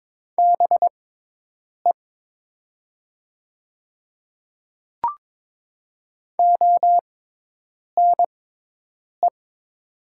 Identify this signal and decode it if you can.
Morse code, heard as a keyed tone, about 22 wpm, spaced as for 8 wpm, 690 Hz. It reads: BE ONE
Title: 6x Character Spacing - Top 100 Words - 22wpm